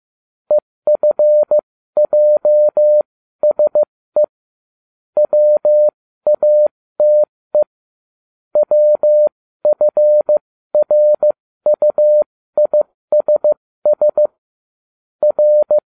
Morse code, heard as a keyed tone, about 15 wpm, spaced as for 13 wpm, 600 Hz. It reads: EFJSE WATE WFRUISS R